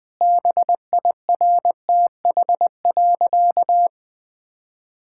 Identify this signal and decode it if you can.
Morse code, heard as a keyed tone, about 21 wpm, 700 Hz. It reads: BIRTH.
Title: Sentences from Top 1000 Words - Review Part 5 - 20wpm